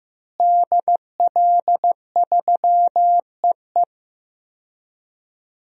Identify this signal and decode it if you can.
Morse code, heard as a keyed tone, about 15 wpm, 700 Hz. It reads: DL3EE
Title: DXCC-100 Call Signs - 5 Characters - Repeated 3x - 15wpm